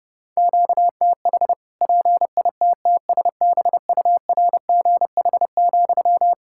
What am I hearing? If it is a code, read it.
Morse code, heard as a keyed tone, about 29 wpm, 700 Hz. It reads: QTH PITTSBURGH,
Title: Practice QSO - Exchange 1 - Beginning - 30wpm